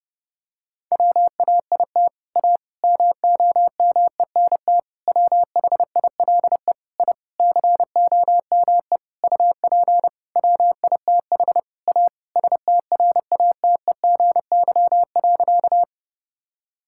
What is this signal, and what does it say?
Morse code, heard as a keyed tone, about 30 wpm, 700 Hz. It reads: WAIT A MOMENT WHILE I COME UP WITH A STRATEGY.
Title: Sentences from Top 1000 Words - Review Part 5 - 30wpm